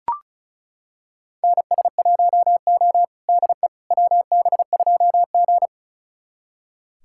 Morse code, heard as a keyed tone, about 34 wpm, 700 Hz. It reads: NS1O DE WB2G
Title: USA Callsign DE Callsign - 4 Characters - 35wpm